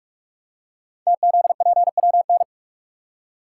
Morse code, heard as a keyed tone, about 44 wpm, 700 Hz. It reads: TZPWN
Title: Code Groups - 5 Letter:Numbers - 45wpm